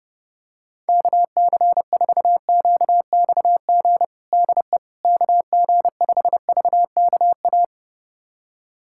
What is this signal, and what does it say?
Morse code, heard as a keyed tone, about 31 wpm, 700 Hz. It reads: KC4QXG DE KG5VKA
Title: USA Callsign DE Callsign - 6 Characters - 30wpm